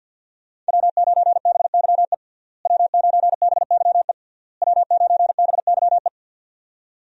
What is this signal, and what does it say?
Morse code, heard as a keyed tone, about 47 wpm, 700 Hz. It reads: W9BYE W9BYE W9BYE
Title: USA Call Signs - 5 Characters - Repeated 3x - 50wpm